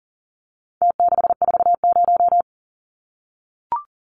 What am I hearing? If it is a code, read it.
Morse code, heard as a keyed tone, about 40 wpm, 700 Hz. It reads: T640
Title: Numbers - Contest Style - 4-Digit 40wpm